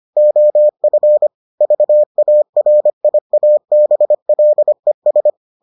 Morse code, heard as a keyed tone, about 25 wpm, 600 Hz. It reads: OF VARIABLES